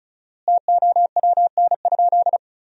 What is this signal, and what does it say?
Morse code, heard as a keyed tone, about 34 wpm, 700 Hz. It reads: TOWN?
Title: 4x Word-Spacing - Sentences from Top 500 Words - 35wpm